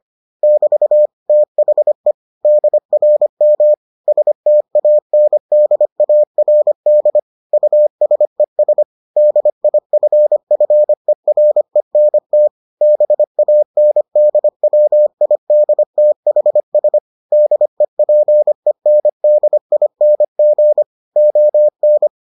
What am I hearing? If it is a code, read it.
Morse code, heard as a keyed tone, about 25 wpm, 600 Hz. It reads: <BT> THE DRM STANDARD USES DIFFERENT BANDWIDTHS DEPENDING ON